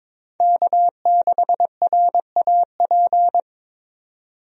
Morse code, heard as a keyed tone, about 22 wpm, 700 Hz. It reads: K6RAP
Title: USA Call Signs - 5 Characters - 22wpm